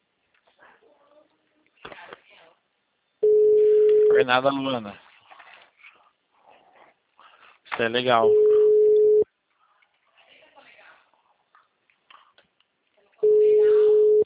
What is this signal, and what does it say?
Ringback tone (Europe, 425 Hz), with speech.
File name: out- -2020- - - .34